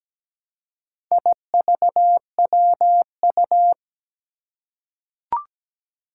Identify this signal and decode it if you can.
Morse code, heard as a keyed tone, about 17 wpm, 700 Hz. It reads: IVWU
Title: Four Challenging Letter-Number Combinations 17wpm